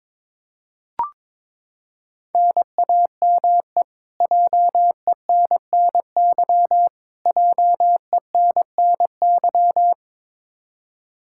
Morse code, heard as a keyed tone, about 22 wpm, 700 Hz. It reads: NAME JENNY JENNY
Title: Practice QSO - Exchange 1 - Beginning - 22wpm